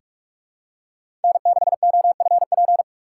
Morse code, heard as a keyed tone, about 46 wpm, 700 Hz. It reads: N6OFP
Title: USA Call Signs - 5 Characters - 3x Character Spacing - 45wpm